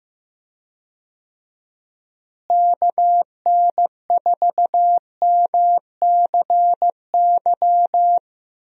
Morse code, heard as a keyed tone, about 15 wpm, 700 Hz. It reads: KN4MCY
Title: USA Call Signs - 6 Characters - Repeated 2x - 15wpm